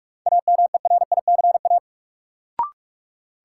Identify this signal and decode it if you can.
Morse code, heard as a keyed tone, about 45 wpm, 700 Hz. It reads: AMERICA
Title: Sets of 5 Words - New Word Drill - 45wpm